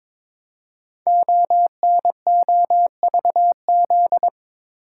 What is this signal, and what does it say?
Morse code, heard as a keyed tone, about 22 wpm, 700 Hz. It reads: ONOVZ
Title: Code Groups - 5 Letters - 22wpm